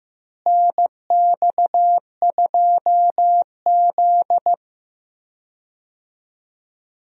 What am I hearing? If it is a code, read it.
Morse code, heard as a keyed tone, about 15 wpm, 700 Hz. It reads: NX2Z
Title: USA Call Signs - 4 Characters - Repeated 3x - 15wpm